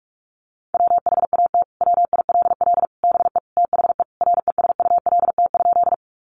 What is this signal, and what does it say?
Morse code, heard as a keyed tone, about 44 wpm, 700 Hz. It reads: WHAT WILL BE THE RESULT?
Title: Sentences from Top 700 Words 45wpm